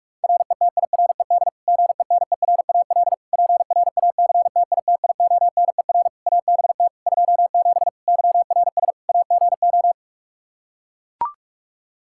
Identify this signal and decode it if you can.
Morse code, heard as a keyed tone, about 41 wpm, 700 Hz. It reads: RETIRED GENERAL PRACTITIONER ABT 27 YRS AGO
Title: Practice QSO - Exchange 3 - More Info - 45wpm